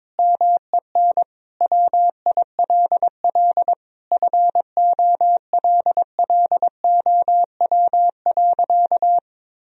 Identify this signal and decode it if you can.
Morse code, heard as a keyed tone, about 22 wpm, 700 Hz. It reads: MEN WILL FOLLOW.
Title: Sentences from Top 1000 Words - Review Part 1 - 22wpm